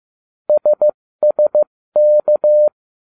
Morse code, heard as a keyed tone, about 15 wpm, spaced as for 13 wpm, 600 Hz. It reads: SSK